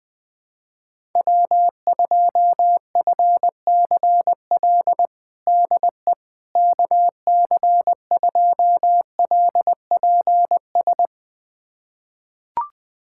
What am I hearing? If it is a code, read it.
Morse code, heard as a keyed tone, about 20 wpm, 700 Hz. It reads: W2FCL DE KC2LPS
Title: USA Callsign DE Callsign 20wpm